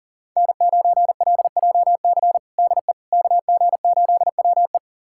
Morse code, heard as a keyed tone, about 37 wpm, 700 Hz. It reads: N9LJC DE KG8WE
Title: USA Callsign DE Callsign - 5 Characters - 40wpm